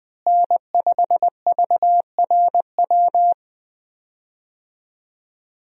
Morse code, heard as a keyed tone, about 21 wpm, 700 Hz. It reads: N5VRW
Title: USA Call Signs - 5 Characters - Repeated 3x - 20wpm